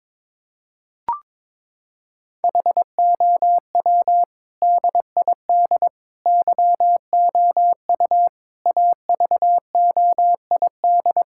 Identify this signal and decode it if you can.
Morse code, heard as a keyed tone, about 21 wpm, 700 Hz. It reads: HOW DID YOU AVOID
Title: Sentences from Top 1000 Words - Review Part 6 - 22wpm